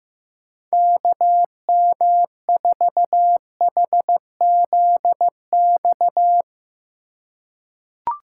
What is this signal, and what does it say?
Morse code, heard as a keyed tone, about 15 wpm, 700 Hz. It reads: KM4HZX
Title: USA Call Signs - 6 Characters - 4x Character Spacing - 15wpm